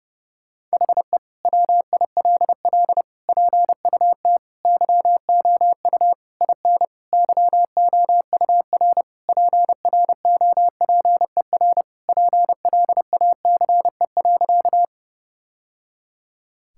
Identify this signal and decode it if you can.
Morse code, heard as a keyed tone, about 30 wpm, 700 Hz. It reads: HE WILL PUT YOU IN YOUR PROPER PLACE.
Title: Sentences from Top 1000 Words - Review Part 5 - 30wpm